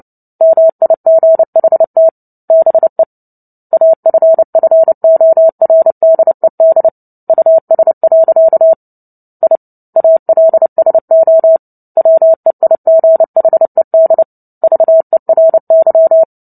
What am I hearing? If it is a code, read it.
Morse code, heard as a keyed tone, about 30 wpm, 650 Hz. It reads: MIGHT BE AFFORDED US. I ALSO WEIGHED VERY
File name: Book Two / 03-Chapter 3 - 29 WPM